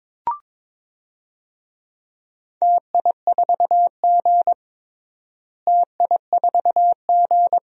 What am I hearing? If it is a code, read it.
Morse code, heard as a keyed tone, about 22 wpm, 700 Hz. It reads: TI4G TI4G
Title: DXCC-100 Call Signs - 4 Characters - Repeated 3x - 22wpm